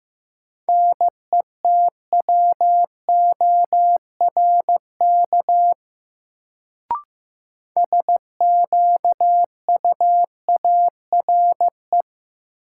Morse code, heard as a keyed tone, about 15 wpm, 700 Hz. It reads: NETWORK SQUARE
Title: Sentences from Top 1000 Words - Review Part 7 - New Word Drill - 15wpm